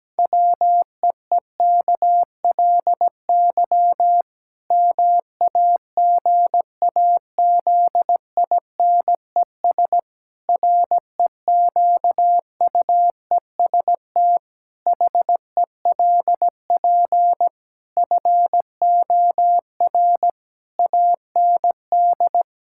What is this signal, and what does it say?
Morse code, heard as a keyed tone, about 17 wpm, 700 Hz. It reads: WEEKLY MAGAZINES REQUEST HELP FOR AND
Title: Pangram Challenge v2 - 17wpm